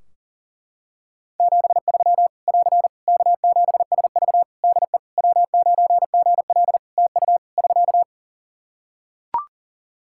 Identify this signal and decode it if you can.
Morse code, heard as a keyed tone, about 40 wpm, 700 Hz. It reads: 73 <AR> K7SV DE W9GL TU <SK>
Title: Practice QSO - Exchange 4 - Ending - 40wpm